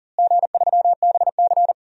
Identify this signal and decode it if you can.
Morse code, heard as a keyed tone, about 40 wpm, 700 Hz. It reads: G3BC
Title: DXCC-100 Call Signs - 4 Characters - Repeated 2x - 40wpm